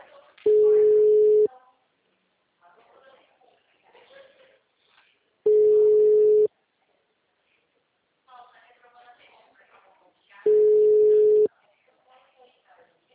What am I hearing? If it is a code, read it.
Ringback tone (Europe, 425 Hz), with speech.
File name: out- -2026- - - .977